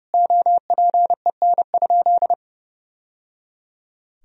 Morse code, heard as a keyed tone, about 32 wpm, 700 Hz. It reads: OPEN?
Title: Sentences from Top 1000 Words - Review Part 1 - 30wpm